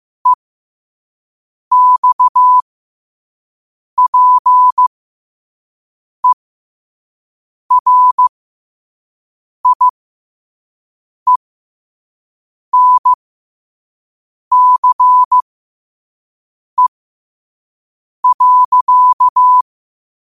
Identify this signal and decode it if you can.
Morse code, heard as a keyed tone, about 15 wpm, spaced as for 5 wpm, 1000 Hz. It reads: EXPERIENCE.